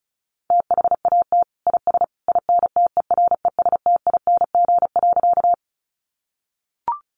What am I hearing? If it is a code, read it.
Morse code, heard as a keyed tone, about 34 wpm, 700 Hz. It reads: THAT IS INTERESTING.
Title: Sentences from Top 1000 Words 35wpm